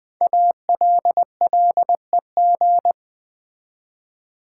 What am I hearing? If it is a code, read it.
Morse code, heard as a keyed tone, about 20 wpm, 700 Hz. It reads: ALLEG
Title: Build A Word - 5 Letters At A Time - 10-Letter Words - 20wpm